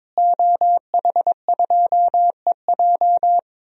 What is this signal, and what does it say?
Morse code, heard as a keyed tone, about 22 wpm, 700 Hz. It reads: OH2EJ
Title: DXCC-100 Call Signs - 5 Characters - Repeated 3x - 22wpm